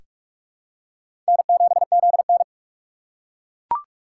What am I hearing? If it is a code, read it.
Morse code, heard as a keyed tone, about 41 wpm, 700 Hz. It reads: N7ZN
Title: K1USN SST QSO Practice - 45wpm